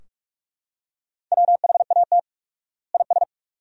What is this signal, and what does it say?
Morse code, heard as a keyed tone, about 46 wpm, 700 Hz. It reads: WHAT IS
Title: 4x Word-Spacing - Sentences from Top 100 Words - 45wpm